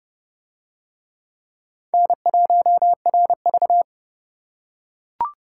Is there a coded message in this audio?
Yes: Morse code from 1.9 to 3.8 s, heard as a keyed tone, about 27 wpm, 700 Hz. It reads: N1RV